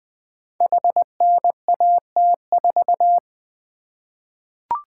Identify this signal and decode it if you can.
Morse code, heard as a keyed tone, about 20 wpm, 700 Hz. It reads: HNAT4
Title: Code Groups - 5 Letter:Numbers - 20wpm